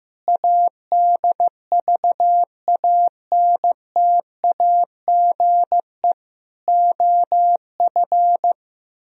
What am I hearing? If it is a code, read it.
Morse code, heard as a keyed tone, about 16 wpm, 700 Hz. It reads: ADVANTAGE OF
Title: Speed-Racing - Sets of 4 Words 15wpm